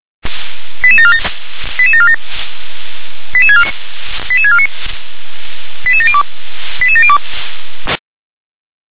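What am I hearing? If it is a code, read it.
Five-tone selective call (ZVEI), heard as quick runs of five tones, tones of about 70 ms. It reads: 88747 88747 88649 88702 88702
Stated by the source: ZVEI five-tone call received by SDR.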